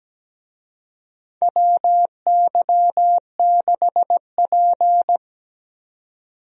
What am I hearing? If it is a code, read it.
Morse code, heard as a keyed tone, about 17 wpm, 700 Hz. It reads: WY6P
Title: USA Call Signs - 4 Characters - 17wpm